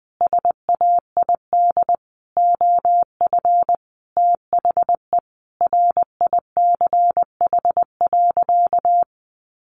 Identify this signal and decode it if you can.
Morse code, heard as a keyed tone, about 20 wpm, 700 Hz. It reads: SAID OF THE RICH.
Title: Sentences from Top 1100 Words 20wpm